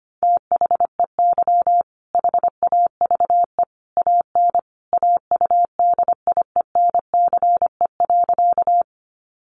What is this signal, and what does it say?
Morse code, heard as a keyed tone, about 25 wpm, 700 Hz. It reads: THEY HAVE AN AUDIENCE.
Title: Sentences from Top 800 Words 25wpm